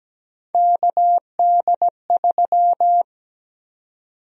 Morse code, heard as a keyed tone, about 17 wpm, 700 Hz. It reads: KD3
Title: USA Call Sign Prefixes - 17wpm